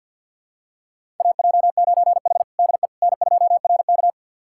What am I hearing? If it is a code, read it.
Morse code, heard as a keyed tone, about 51 wpm, 700 Hz. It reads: AJ9H DE N2RK
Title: USA Callsign DE Callsign - 4 Characters - 50wpm